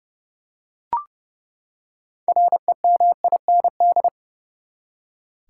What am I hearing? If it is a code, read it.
Morse code, heard as a keyed tone, about 30 wpm, 700 Hz. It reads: REMIND
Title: Six Letter Words - 30wpm